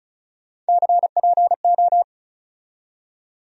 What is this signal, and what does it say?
Morse code, heard as a keyed tone, about 36 wpm, 700 Hz. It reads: CPO
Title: USA Call Sign Postfixes - 3 Letters 35wpm